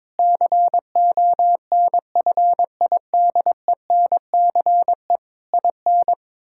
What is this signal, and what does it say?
Morse code, heard as a keyed tone, about 22 wpm, 700 Hz. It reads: CONFIDENCE IN
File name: Difficult Sentences with Hints - 22wpm